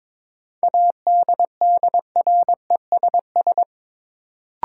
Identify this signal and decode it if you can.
Morse code, heard as a keyed tone, about 22 wpm, 700 Hz. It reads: ADDRESS
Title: Sets of 5 Words Encore - New Word Drill - 22wpm